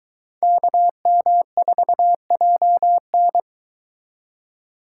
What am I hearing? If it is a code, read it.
Morse code, heard as a keyed tone, about 23 wpm, 700 Hz. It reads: KM4JN